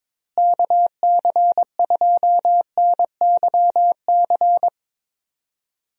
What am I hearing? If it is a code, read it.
Morse code, heard as a keyed tone, about 23 wpm, 700 Hz. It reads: KC2NYC
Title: USA Call Signs - 6 Characters - Repeated 3x - 22wpm